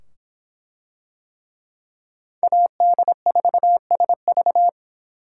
Morse code, heard as a keyed tone, about 23 wpm, 700 Hz. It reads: AD4SV